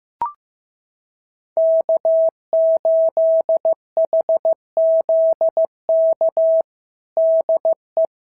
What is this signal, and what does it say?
Morse code, heard as a keyed tone, about 14 wpm, 650 Hz. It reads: K8HZK DE
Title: SOTA QSO Practice 15wpm